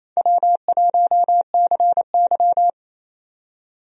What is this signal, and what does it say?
Morse code, heard as a keyed tone, about 28 wpm, 700 Hz. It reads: W1CY
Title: USA Call Signs - 4 Characters - 28wpm